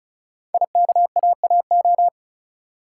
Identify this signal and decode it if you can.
Morse code, heard as a keyed tone, about 36 wpm, 700 Hz. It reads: IKAAO
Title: Code Groups - 5 Letters - 35wpm